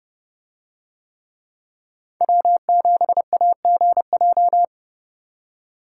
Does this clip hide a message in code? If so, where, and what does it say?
Yes: Morse code from 2.2 to 4.6 s, heard as a keyed tone, about 31 wpm, 700 Hz. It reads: W7AGJ